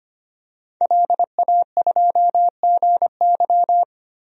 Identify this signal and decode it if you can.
Morse code, heard as a keyed tone, about 25 wpm, 700 Hz. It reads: LA2GY
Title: DXCC-100 Call Signs - 5 Characters - 25wpm